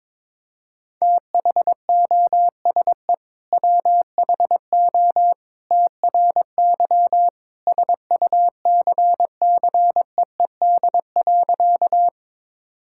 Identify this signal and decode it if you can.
Morse code, heard as a keyed tone, about 22 wpm, 700 Hz. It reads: THOSE WHO TRY SUCCEED.
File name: practice / 22wpm / Sentences with 2-Word Phrases - 22wpm